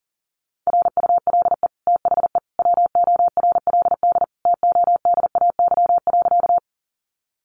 Morse code, heard as a keyed tone, about 40 wpm, 700 Hz. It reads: RULE THE WORLD TODAY.
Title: Sentences from Top 600 Words 40wpm